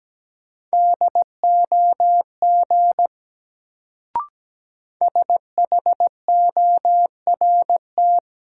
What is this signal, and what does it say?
Morse code, heard as a keyed tone, about 17 wpm, 700 Hz. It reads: DOG SHORT
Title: Sentences from Top 400 Words - New Word Drill - 17wpm